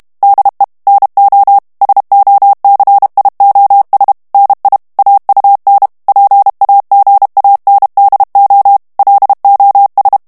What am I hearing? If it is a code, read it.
Morse code, heard as a keyed tone, about 32 wpm, 800 Hz. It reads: DE NO SOCIOS NI AUN PAGANDO LOS
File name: morse mp3 / ESPAGNOL TEXTE EN CLAIR 1 VITESSE 30 MOTS MINUTES